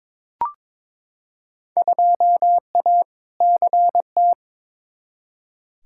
Morse code, heard as a keyed tone, about 22 wpm, 700 Hz. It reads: 2A CT
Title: Field Day Exchanges Encore - 22wpm